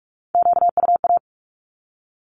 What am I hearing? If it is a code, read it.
Morse code, heard as a keyed tone, about 44 wpm, 700 Hz. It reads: QUA
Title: Q-Codes 45wpm